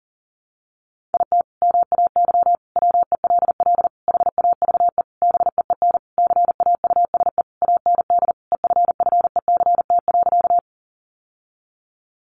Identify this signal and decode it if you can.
Morse code, heard as a keyed tone, about 40 wpm, 700 Hz. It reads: IT MAY WELL HAVE BEEN CAUSE AND EFFECT.
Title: Sentences from Top 1000 Words 40wpm